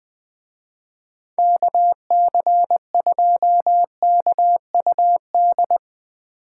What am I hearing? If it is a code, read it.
Morse code, heard as a keyed tone, about 22 wpm, 700 Hz. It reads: KC2KUD
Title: Speed Racing - USA 2x3 Call Signs 20wpm